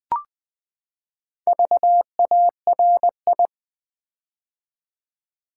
Morse code, heard as a keyed tone, about 20 wpm, 700 Hz. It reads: VARI
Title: Build A Word - 4 Letters At A Time - 8-Letter Words - 20wpm